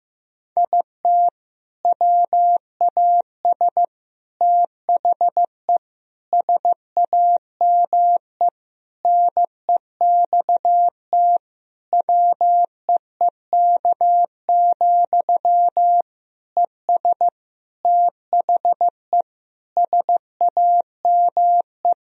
Morse code, heard as a keyed tone, about 15 wpm, 700 Hz. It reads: IT WAS THE SAME NEXT WEEK, ES THE SAME
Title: Sentences from Top 500 Words and Abbreviations 15wpm